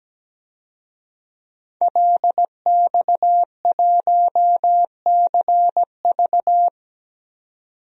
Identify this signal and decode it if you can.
Morse code, heard as a keyed tone, about 17 wpm, 700 Hz. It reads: LX1CV
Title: DXCC-100 Call Signs - 5 Characters - 17wpm